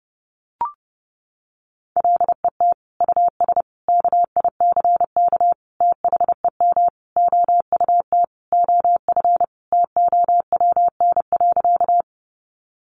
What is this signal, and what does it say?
Morse code, heard as a keyed tone, about 30 wpm, 700 Hz. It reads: LET US KICK THEM OUT OF TOWN.
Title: Sentences from Top 1300 Words 30wpm